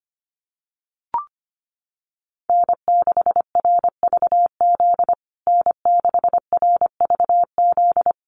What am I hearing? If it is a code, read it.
Morse code, heard as a keyed tone, about 25 wpm, 700 Hz. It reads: N6RVZ N6RVZ
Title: USA Call Signs Repeated 3x 25wpm